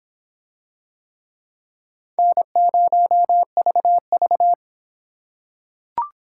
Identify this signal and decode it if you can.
Morse code, heard as a keyed tone, about 25 wpm, 700 Hz. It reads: N0VV